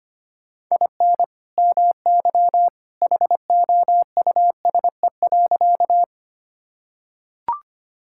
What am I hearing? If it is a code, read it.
Morse code, heard as a keyed tone, about 25 wpm, 700 Hz. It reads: IN MY HOUSE.
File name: practice / 25wpm / Sentences with 2-Word Phrases - 25wpm